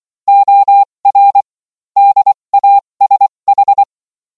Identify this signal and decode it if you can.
Morse code, heard as a keyed tone, about 24 wpm, spaced as for 20 wpm, 800 Hz. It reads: OR DASH